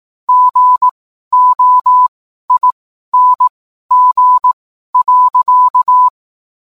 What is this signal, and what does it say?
Morse code, heard as a keyed tone, about 18 wpm, spaced as for 12 wpm, 1000 Hz. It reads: GOING.